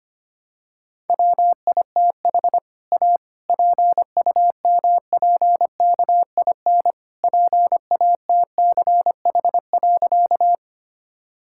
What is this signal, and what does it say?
Morse code, heard as a keyed tone, about 25 wpm, 700 Hz. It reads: WITH A PUMPKIN PATCH.
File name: practice / 25wpm / All-Age Appropriate Jokes - 25wpm